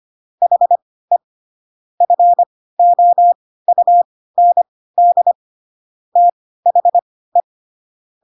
Morse code, heard as a keyed tone, about 25 wpm, spaced as for 16 wpm, 700 Hz. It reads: HE FOUND THE
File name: Pinocchio / Pinocchio ch4 10 25 ik1hge